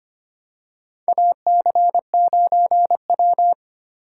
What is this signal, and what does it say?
Morse code, heard as a keyed tone, about 25 wpm, 700 Hz. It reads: AC9W
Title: USA Call Signs - 4 Characters - 4x Character Spacing - 25wpm